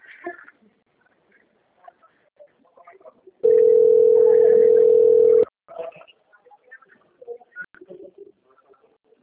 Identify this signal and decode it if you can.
Ringback tone (North America), 440 + 480 Hz, with speech.
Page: Recife - PE